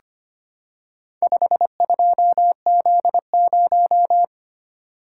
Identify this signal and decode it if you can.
Morse code, heard as a keyed tone, about 25 wpm, 700 Hz. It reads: 52Z0
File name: practice / 25wpm / Four Letter-Number Combinations - 25wpm